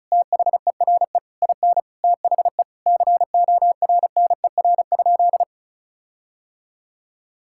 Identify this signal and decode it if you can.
Morse code, heard as a keyed tone, about 34 wpm, 700 Hz. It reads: THERE IN THE CORNER?
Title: Sentences from Top 1600 Words 35wpm